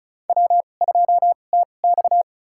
Morse code, heard as a keyed tone, about 36 wpm, spaced as for 25 wpm, 700 Hz. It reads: W2TX
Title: USA Call Signs - 4 Characters - 2x Character Spacing - 35wpm